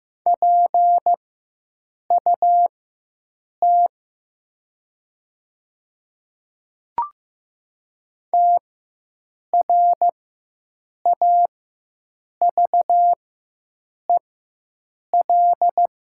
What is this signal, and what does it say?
Morse code, heard as a keyed tone, about 15 wpm, spaced as for 7 wpm, 700 Hz. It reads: PUT TRAVEL
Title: 4x Character Spacing - Top 500 Words - 15wpm